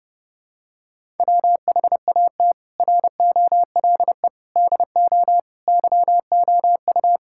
Morse code, heard as a keyed tone, about 30 wpm, 700 Hz. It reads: WHAT ROLE DO YOU
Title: Sentences from Top 1000 Words - Review Part 6 - 30wpm